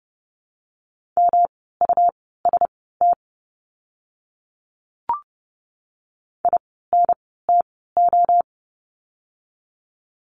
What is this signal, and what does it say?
Morse code, heard as a keyed tone, about 30 wpm, spaced as for 17 wpm, 700 Hz. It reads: MUST INTO
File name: course / 30:10wpm / 237 Lesson 38f - Letter q - Words 30wpm